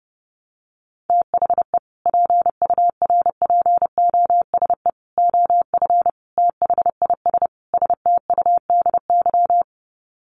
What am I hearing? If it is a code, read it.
Morse code, heard as a keyed tone, about 28 wpm, 700 Hz. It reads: THE PURPOSE OF THIS STUDY
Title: Sets of 5 Words 30wpm